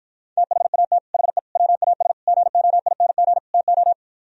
Morse code, heard as a keyed tone, about 54 wpm, 700 Hz. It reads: THAT HE WAS GOING TO